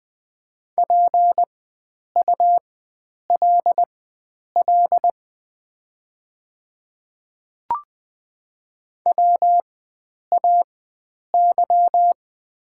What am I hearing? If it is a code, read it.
Morse code, heard as a keyed tone, about 20 wpm, spaced as for 9 wpm, 700 Hz. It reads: PULL WAY
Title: 4x Character Spacing - Top 500 Words - 20wpm